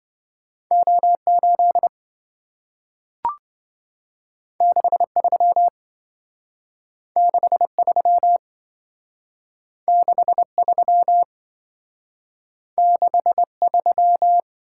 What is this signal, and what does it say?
Morse code, heard as a keyed tone, about 27 wpm, 700 Hz. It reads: O8 63 63 63 63